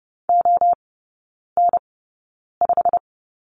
Morse code, heard as a keyed tone, about 30 wpm, spaced as for 9 wpm, 700 Hz. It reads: ON5
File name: course / 30:10wpm / 087 Lesson 15e - Number 5 - Three Characters 30wpm